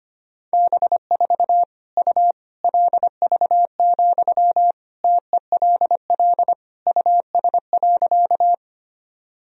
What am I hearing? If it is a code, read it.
Morse code, heard as a keyed tone, about 25 wpm, 700 Hz. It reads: B4 U LV, TELL US.
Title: Sentences from Top 400 Words and Abbreviations 25wpm